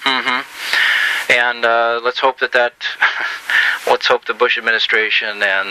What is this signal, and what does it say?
Emergency Alert System (SAME) tones, heard as harsh buzzing bursts, with speech. It is heard from 1.2 to 5.0 s.